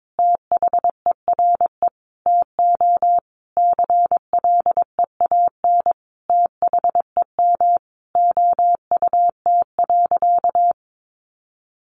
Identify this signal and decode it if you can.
Morse code, heard as a keyed tone, about 22 wpm, 700 Hz. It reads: THERE TO CLEAN THEM OUT.
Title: Sentences from Top 1500 Words 22wpm